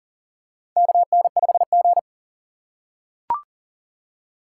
Morse code, heard as a keyed tone, about 42 wpm, 700 Hz. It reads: KN5G
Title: USA Call Signs - 4 Characters - 3x Character Spacing - 40wpm